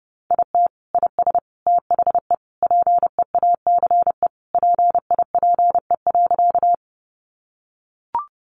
Morse code, heard as a keyed tone, about 30 wpm, 700 Hz. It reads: IT IS THE PEACE PIPE.
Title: Sentences from Top 1500 Words 30wpm